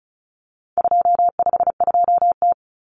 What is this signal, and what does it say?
Morse code, heard as a keyed tone, about 34 wpm, 700 Hz. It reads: 252T
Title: Common Cut Numbers - 4-Digit 35wpm